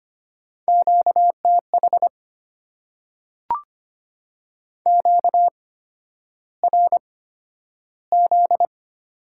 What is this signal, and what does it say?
Morse code, heard as a keyed tone, about 25 wpm, spaced as for 11 wpm, 700 Hz. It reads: QTH QRZ
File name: Top 90 QSO Elements - 8x Character Spacing - 25wpm